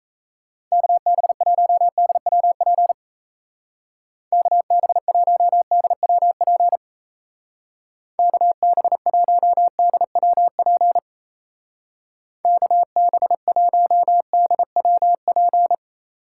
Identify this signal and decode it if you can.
Morse code, heard as a keyed tone, about 34 wpm, 700 Hz. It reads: KB1DWP KB1DWP KB1DWP KB1DWP
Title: Speed Racing - USA Call Signs 28wpm